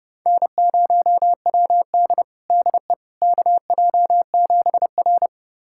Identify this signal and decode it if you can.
Morse code, heard as a keyed tone, about 32 wpm, 700 Hz. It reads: N0WD DE KJ7R
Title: USA Callsign DE Callsign - 4 Characters - 30wpm